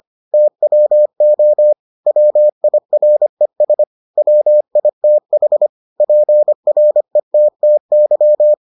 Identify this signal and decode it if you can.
Morse code, heard as a keyed tone, about 25 wpm, 600 Hz. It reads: TWO WIRES WITH PRETTY